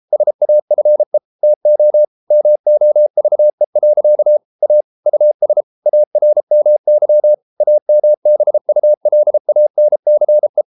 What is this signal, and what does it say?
Morse code, heard as a keyed tone, about 33 wpm, 600 Hz. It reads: SAFE TO MOVE. A US ARMY AMBULANCE